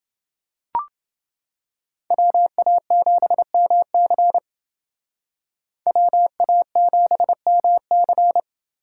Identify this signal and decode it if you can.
Morse code, heard as a keyed tone, about 28 wpm, 700 Hz. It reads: WA7MC WA7MC